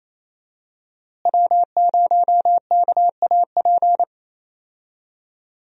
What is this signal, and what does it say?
Morse code, heard as a keyed tone, about 27 wpm, 700 Hz. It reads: W0KAP
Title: USA Call Signs - 5 Characters - Repeated 3x - 28wpm